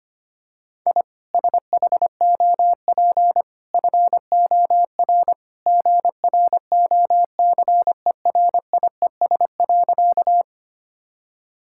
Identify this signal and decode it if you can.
Morse code, heard as a keyed tone, about 25 wpm, 690 Hz. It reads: I SHOP FOR GROCERIES.
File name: Easy Sentences with Hints - 25wpm